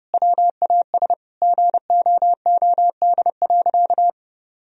Morse code, heard as a keyed tone, about 32 wpm, 700 Hz. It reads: WAS GOOD.
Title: Sentences from Top 1000 Words - Review Part 2 - 30wpm